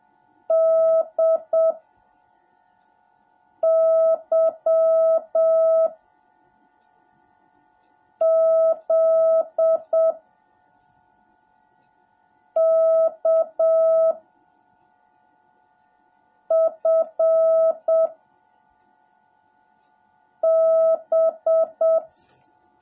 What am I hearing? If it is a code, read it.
Morse code, heard as a keyed tone, about 7 wpm, 650 Hz. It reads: D Y Z K F B
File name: jh8cbh / 2021cw / cw17a